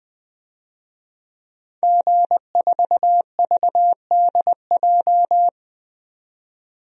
Morse code, heard as a keyed tone, about 20 wpm, 700 Hz. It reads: G4VDJ
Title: DXCC-100 Call Signs - 5 Characters - 20wpm